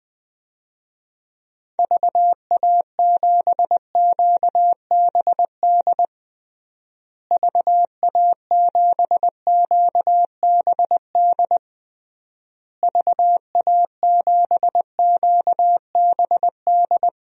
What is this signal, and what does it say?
Morse code, heard as a keyed tone, about 20 wpm, 700 Hz. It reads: VA7QBD VA7QBD VA7QBD
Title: DXCC-100 Call Signs - 6 Characters - Repeated 3x - 20wpm